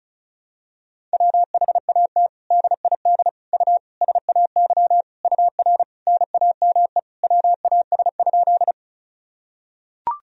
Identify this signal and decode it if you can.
Morse code, heard as a keyed tone, about 34 wpm, 700 Hz. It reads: WHAT DID U SAY UR NAME WAS?
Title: Sentences from Top 200 Words and Abbreviations 35wpm